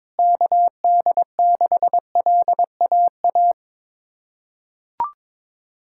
Morse code, heard as a keyed tone, about 22 wpm, 700 Hz. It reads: KD6LAA
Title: USA Callsign DE Callsign - 6 Characters - 22wpm